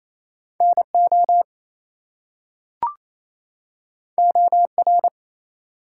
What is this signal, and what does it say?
Morse code, heard as a keyed tone, about 28 wpm, 700 Hz. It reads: NO OR
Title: Two Letter Words - 28wpm